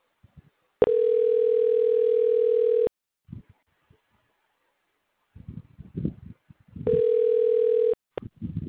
Ringback tone (North America), 440 + 480 Hz.
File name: recordings / REf8336bf147caf c5d d